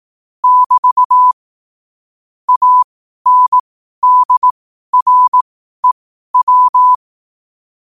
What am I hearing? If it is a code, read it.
Morse code, heard as a keyed tone, about 18 wpm, spaced as for 12 wpm, 1000 Hz. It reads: <BT> ANDREW
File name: qotdfiles / 12wpm